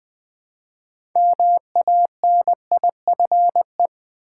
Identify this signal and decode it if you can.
Morse code, heard as a keyed tone, about 20 wpm, 700 Hz. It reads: MANIFE
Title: Build A Word - 2 Letters At A Time - 8-Letter Words - 20wpm